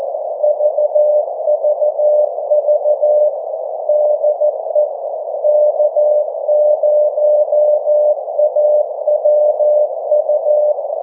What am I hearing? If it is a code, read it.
Morse code, heard as a keyed tone, about 14 wpm, 600 Hz. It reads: VVV DE K0AWU
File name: 1899khz / k0awu 1742ut-24jan21